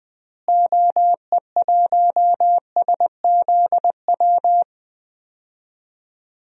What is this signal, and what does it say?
Morse code, heard as a keyed tone, about 21 wpm, 700 Hz. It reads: OE1SZW
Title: DXCC-100 Call Signs - 6 Characters - Repeated 3x - 20wpm